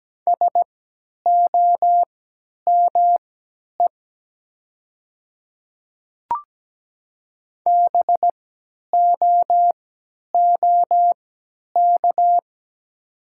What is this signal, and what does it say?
Morse code, heard as a keyed tone, about 17 wpm, spaced as for 10 wpm, 700 Hz. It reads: SOME BOOK
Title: 3x Character Spacing - Top 300 Words - 17wpm